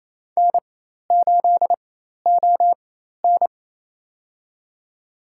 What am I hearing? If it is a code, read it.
Morse code, heard as a keyed tone, about 27 wpm, spaced as for 13 wpm, 700 Hz. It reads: N8ON
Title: USA Call Signs - 4 Characters - 4x Character Spacing - 28wpm